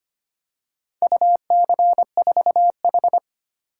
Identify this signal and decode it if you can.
Morse code, heard as a keyed tone, about 25 wpm, 700 Hz. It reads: UC4H
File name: practice / 25wpm / Four Letter-Number Combinations - 25wpm